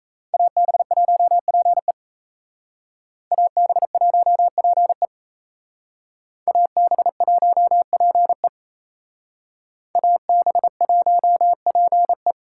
Morse code, heard as a keyed tone, about 35 wpm, 700 Hz. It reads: AB1PE AB1PE AB1PE AB1PE
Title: Speed Racing - USA 2x2 Call Signs 28wpm